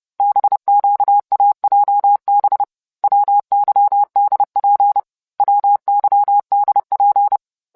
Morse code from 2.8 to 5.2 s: WYDP